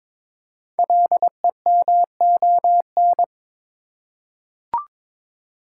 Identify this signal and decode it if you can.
Morse code, heard as a keyed tone, about 23 wpm, 700 Hz. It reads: LEMON
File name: 5 Letter Words - First 3 Letters Given - 22wpm